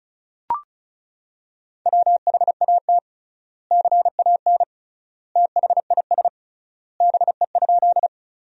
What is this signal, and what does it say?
Morse code, heard as a keyed tone, about 34 wpm, 700 Hz. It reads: WHAT CAN THIS BE?
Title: 3x Word-Spacing - Sentences from Top 100 Words - 35wpm